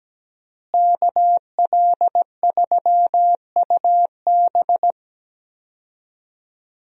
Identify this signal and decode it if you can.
Morse code, heard as a keyed tone, about 17 wpm, 700 Hz. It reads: KL3UB
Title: USA Call Signs - 5 Characters - Repeated 2x - 17wpm